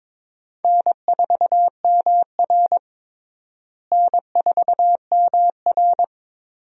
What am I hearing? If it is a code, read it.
Morse code, heard as a keyed tone, about 22 wpm, 700 Hz. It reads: N4MR N4MR
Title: USA Call Signs - 4 Characters - Repeated 3x - 22wpm